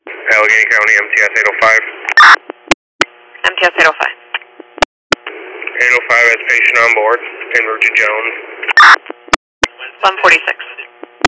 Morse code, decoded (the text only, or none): none